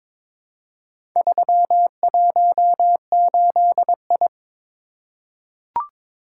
Morse code, heard as a keyed tone, about 21 wpm, 690 Hz. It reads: 318I
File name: practice / 22wpm / Four Letter-Number Combinations - 22wpm